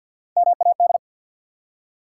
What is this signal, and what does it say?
Morse code, heard as a keyed tone, about 54 wpm, 700 Hz. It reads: MAD
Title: Three Letter Words - 50wpm